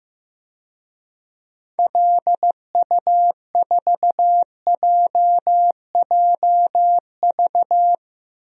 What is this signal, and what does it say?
Morse code, heard as a keyed tone, about 15 wpm, 700 Hz. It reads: LU4JJV
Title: DXCC-100 Call Signs - 6 Characters - Repeated 3x - 15wpm